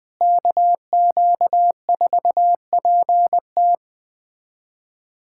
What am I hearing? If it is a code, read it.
Morse code, heard as a keyed tone, about 21 wpm, 700 Hz. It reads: KQ4PT
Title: USA Callsign DE Callsign - 5 Characters - 20wpm